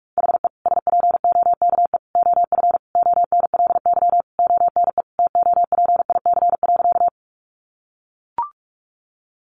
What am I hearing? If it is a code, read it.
Morse code, heard as a keyed tone, about 44 wpm, 700 Hz. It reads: HE SPOKE OF ONLY ONE TOPIC.
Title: Sentences from Top 1200 Words 45wpm